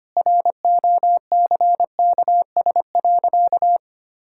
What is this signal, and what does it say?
Morse code, heard as a keyed tone, about 25 wpm, 700 Hz. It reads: ROCKS.
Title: Pangram Challenge v2 - 25wpm